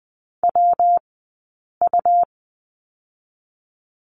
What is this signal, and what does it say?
Morse code, heard as a keyed tone, about 20 wpm, 700 Hz. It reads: W U
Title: 120 Lesson 20d - Number 3 - Two Characters 20wpm